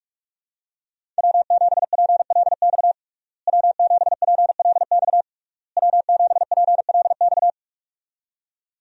Morse code, heard as a keyed tone, about 46 wpm, 700 Hz. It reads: W7PLX W7PLX W7PLX
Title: USA Call Signs - 5 Characters - Repeated 3x - 45wpm